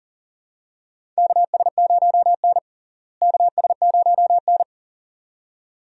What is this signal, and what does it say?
Morse code, heard as a keyed tone, about 40 wpm, 700 Hz. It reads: KS0N KS0N
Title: USA Call Signs - 4 Characters - Repeated 2x - 40wpm